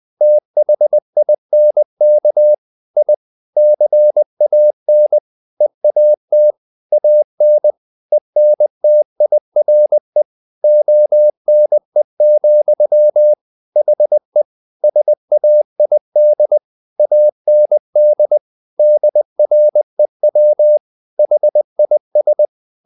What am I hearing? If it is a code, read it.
Morse code, heard as a keyed tone, about 20 wpm, 600 Hz. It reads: THINK I CAN EAT AN ENTIRE ONE, HE SAID AND DREW HIS
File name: the old man and the sea / the old man and the sea part 1 20wpm